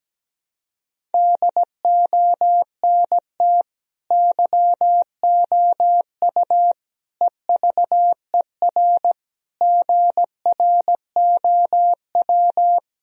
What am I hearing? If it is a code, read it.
Morse code, heard as a keyed tone, about 17 wpm, 700 Hz. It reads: DONT YOU EVER GROW
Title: Sentences from Top 1000 Words - Review Part 1 - 17wpm